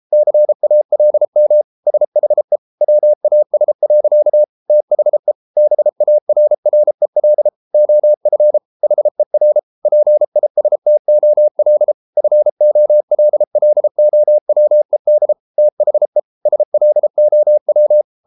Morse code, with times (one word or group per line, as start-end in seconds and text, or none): 0.1-1.6 CALM
1.9-2.6 SHE
2.8-4.4 WAS.
4.7-5.3 THE
5.6-7.5 BARREL
7.7-8.6 OF
8.8-9.6 HER
9.9-11.9 PISTOL
12.2-15.3 FOLLOWED
15.6-16.2 THE
16.5-18.0 SLOW